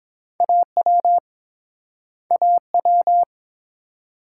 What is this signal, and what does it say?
Morse code, heard as a keyed tone, about 28 wpm, 700 Hz. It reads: AW AW